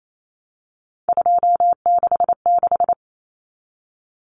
Morse code, heard as a keyed tone, about 28 wpm, 700 Hz. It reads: 266
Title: Common Cut Numbers - 3-Digit 28wpm